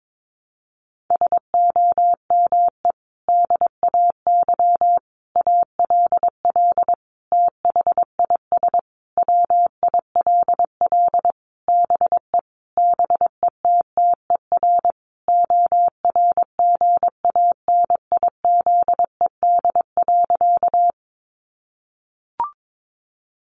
Morse code, heard as a keyed tone, about 22 wpm, 700 Hz. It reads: SOME DAY ALL THIS WILL BE BETTER ORGANIZED.
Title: Sentences from Top 1300 Words 22wpm